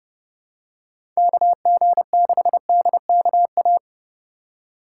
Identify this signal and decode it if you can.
Morse code, heard as a keyed tone, about 30 wpm, 700 Hz. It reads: KG6DKA
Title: USA Call Signs - 6 Characters - 8x Character Spacing - 30wpm